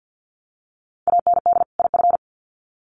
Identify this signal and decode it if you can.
Morse code, heard as a keyed tone, about 50 wpm, 700 Hz. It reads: AND IF
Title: Sets of 2 Words 50wpm